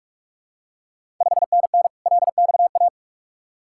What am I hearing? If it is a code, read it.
Morse code, heard as a keyed tone, about 45 wpm, 690 Hz. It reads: 5NN LXA
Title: Ontario QSO Party Exchange 45wpm